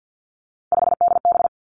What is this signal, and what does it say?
Morse code, heard as a keyed tone, about 50 wpm, 700 Hz. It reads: 5DB
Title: Three Challenging Letter-Number Combinations 50wpm